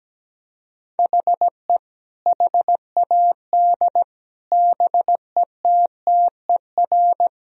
Morse code, heard as a keyed tone, about 17 wpm, 700 Hz. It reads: HE HAD BETTER
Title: Sentences from Top 1000 Words - Review Part 2 - 17wpm